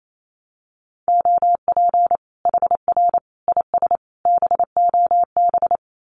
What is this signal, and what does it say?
Morse code, heard as a keyed tone, about 28 wpm, 700 Hz. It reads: OP HR IS BOB
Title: Common QSO Phrases 28wpm